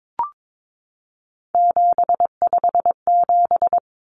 Morse code, heard as a keyed tone, about 22 wpm, 700 Hz. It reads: 757
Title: Common Cut Numbers - 3-Digit 22wpm